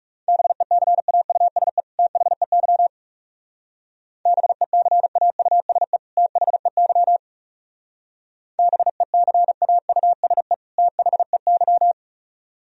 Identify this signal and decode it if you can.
Morse code, heard as a keyed tone, about 37 wpm, 700 Hz. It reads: BECAUSE THEY BECAUSE THEY BECAUSE THEY
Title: Speed-Racing - Sets of 2 Words 30wpm